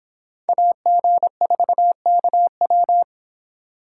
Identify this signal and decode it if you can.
Morse code, heard as a keyed tone, about 26 wpm, 700 Hz. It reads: AG4KW